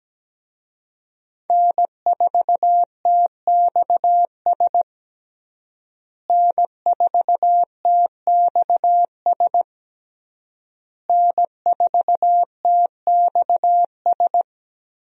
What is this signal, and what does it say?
Morse code, heard as a keyed tone, about 17 wpm, 700 Hz. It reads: N4TXS N4TXS N4TXS
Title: USA Call Signs - 5 Characters - Repeated 3x - 17wpm